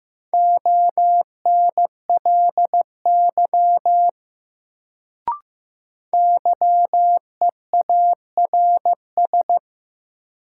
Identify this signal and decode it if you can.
Morse code, heard as a keyed tone, about 15 wpm, 700 Hz. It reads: ONLY YEARS
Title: Sets of 2 Words - New Word Drill - 15wpm